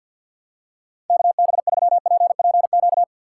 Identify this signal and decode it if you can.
Morse code, heard as a keyed tone, about 53 wpm, 700 Hz. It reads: KB3PPQ
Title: USA Call Signs - 6 Characters - 8x Character Spacing - 50wpm